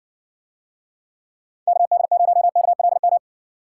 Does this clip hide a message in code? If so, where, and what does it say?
Yes: Morse code from 1.7 to 3.2 s, heard as a keyed tone, about 57 wpm, 700 Hz. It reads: KD0KLM